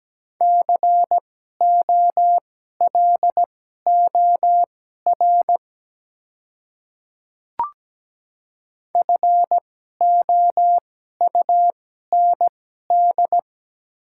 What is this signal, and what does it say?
Morse code, heard as a keyed tone, about 17 wpm, spaced as for 12 wpm, 700 Hz. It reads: COLOR FOUND
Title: 2x Character Spacing - Top 300 Words - 17wpm